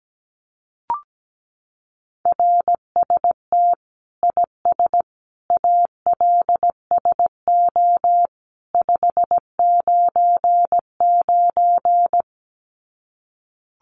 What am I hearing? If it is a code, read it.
Morse code, heard as a keyed tone, about 17 wpm, 700 Hz. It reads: RST IS ALSO 599
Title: Common QSO Phrases 17wpm